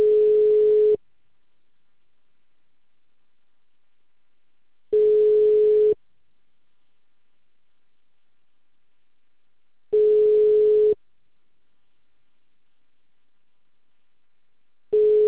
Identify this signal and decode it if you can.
Ringback tone (Europe, 425 Hz).